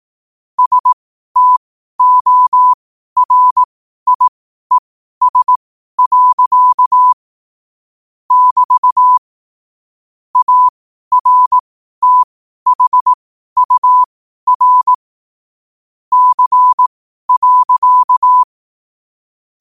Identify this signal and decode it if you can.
Morse code, heard as a keyed tone, about 18 wpm, spaced as for 12 wpm, 1000 Hz. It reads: STORIES. <BT> ARTHUR C.
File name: qotdfiles / 12wpm